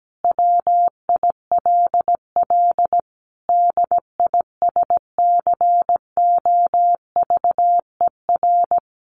Morse code, heard as a keyed tone, about 17 wpm, 700 Hz. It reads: WILL DISCOVER
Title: Sentences from Top 1100 Words 17wpm